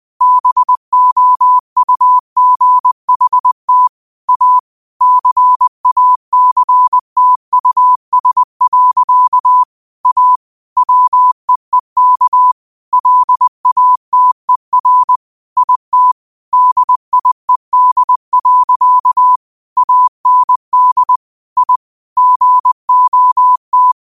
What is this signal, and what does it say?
Morse code, heard as a keyed tone, about 20 wpm, 1000 Hz. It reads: BOUGHT A CACTUS. A WEEK LATER IT DIED. AND I GOT
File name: 20wpm